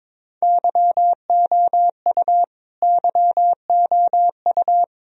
Morse code, heard as a keyed tone, about 22 wpm, 700 Hz. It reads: YOU YOU
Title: Top 100 Words Repeated 22wpm